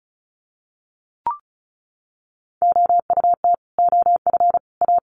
Morse code, heard as a keyed tone, about 34 wpm, 700 Hz. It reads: OUT OF A
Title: Sets of 3 Words 35wpm